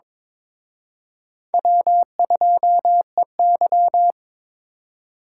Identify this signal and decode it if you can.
Morse code, heard as a keyed tone, about 26 wpm, 700 Hz. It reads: W2EY
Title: Speed Racing - USA Call Signs 22wpm